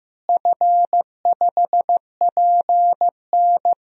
Morse code, heard as a keyed tone, about 15 wpm, 700 Hz. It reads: F5PN
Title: DXCC-100 Call Signs - 4 Characters - Repeated 2x - 15wpm